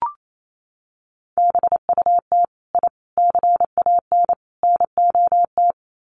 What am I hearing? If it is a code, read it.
Morse code, heard as a keyed tone, about 28 wpm, 700 Hz. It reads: BUT I CAN NOT
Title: Sets of 4 Words 28wpm